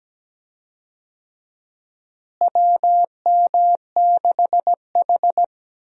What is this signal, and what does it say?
Morse code, heard as a keyed tone, about 17 wpm, 700 Hz. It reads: WM6H
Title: USA Call Signs - 4 Characters - Repeated 2x - 17wpm